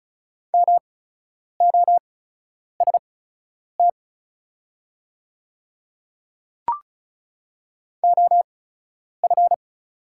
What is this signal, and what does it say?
Morse code, heard as a keyed tone, about 34 wpm, spaced as for 10 wpm, 690 Hz. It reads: MOST OF
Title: 8x Character Spacing - Top 100 Words - 35wpm